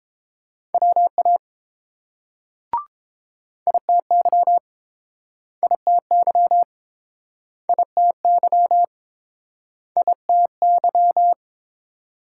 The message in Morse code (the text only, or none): WA ITY ITY ITY ITY